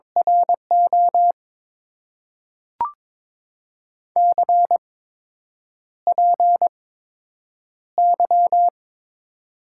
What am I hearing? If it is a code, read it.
Morse code, heard as a keyed tone, about 22 wpm, spaced as for 8 wpm, 700 Hz. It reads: RO CPY
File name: Top 90 QSO Elements - 8x Character Spacing - 22wpm